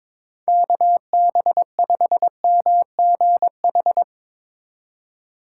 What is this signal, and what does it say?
Morse code, heard as a keyed tone, about 22 wpm, 700 Hz. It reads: KB5MGH
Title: USA Call Signs - 6 Characters - 4x Character Spacing - 22wpm